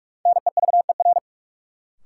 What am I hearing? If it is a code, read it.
Morse code, heard as a keyed tone, about 41 wpm, 700 Hz. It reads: NEVER
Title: Sentences from Top 300 Words - New Word Drill - 45wpm